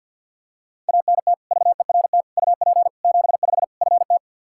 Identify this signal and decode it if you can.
Morse code, heard as a keyed tone, about 50 wpm, 700 Hz. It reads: ANT VERT UP 75 FT
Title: K1USN SST QSO Practice 50wpm